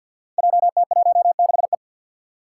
Morse code, heard as a keyed tone, about 47 wpm, 700 Hz. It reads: JT16E
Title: Five Challenging Letter-Number Combinations 50wpm